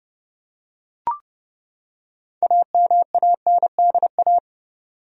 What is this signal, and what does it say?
Morse code, heard as a keyed tone, about 30 wpm, 700 Hz. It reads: AMANDA
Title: USA Common Names - 30wpm